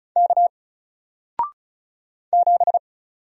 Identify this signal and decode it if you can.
Morse code, heard as a keyed tone, about 36 wpm, 700 Hz. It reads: K 7
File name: Single-Letter-Number-35wpm